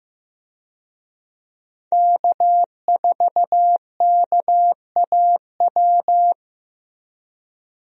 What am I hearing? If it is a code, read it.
Morse code, heard as a keyed tone, about 15 wpm, 700 Hz. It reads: K4KAW
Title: USA Call Signs - 5 Characters - Repeated 2x - 15wpm